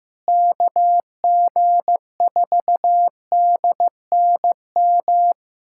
Morse code, heard as a keyed tone, about 15 wpm, 700 Hz. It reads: KG4DNM
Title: USA Callsign DE Callsign 15wpm